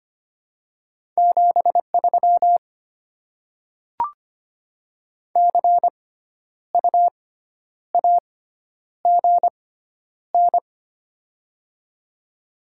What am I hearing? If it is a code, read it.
Morse code, heard as a keyed tone, about 25 wpm, spaced as for 10 wpm, 700 Hz. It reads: 73 CU AGN
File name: Top 20 QSO Elements - 6x Character Spacing - 25wpm